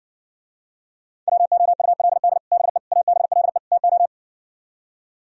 Morse code, heard as a keyed tone, about 60 wpm, 700 Hz. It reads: WOULD BE ABLE TO